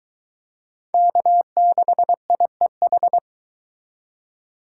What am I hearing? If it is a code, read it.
Morse code, heard as a keyed tone, about 22 wpm, 700 Hz. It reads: K6IEH